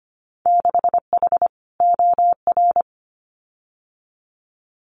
Morse code, heard as a keyed tone, about 25 wpm, 700 Hz. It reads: 6H OR
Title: Winter Field Day Exchanges Encore 25wpm